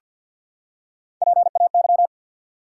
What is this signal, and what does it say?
Morse code, heard as a keyed tone, about 51 wpm, 700 Hz. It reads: PAY
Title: Three Letter Words - 50wpm